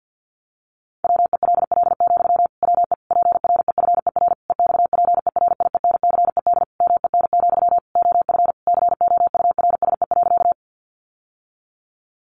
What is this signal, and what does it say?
Morse code, heard as a keyed tone, about 50 wpm, 700 Hz. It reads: WELL, WE PREFER EXPERIENCED MEN, OF COURSE.
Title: Sentences from Top 1100 Words 50wpm